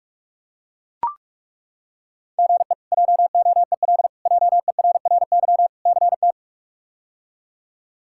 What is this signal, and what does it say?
Morse code, heard as a keyed tone, about 43 wpm, 700 Hz. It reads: GE JOEL JERRY CT
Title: K1USN SST QSO Practice - 45wpm